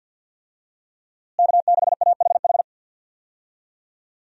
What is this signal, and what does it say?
Morse code, heard as a keyed tone, about 49 wpm, 700 Hz. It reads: K6AHH
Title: USA Call Signs - 5 Characters - Repeated 3x - 50wpm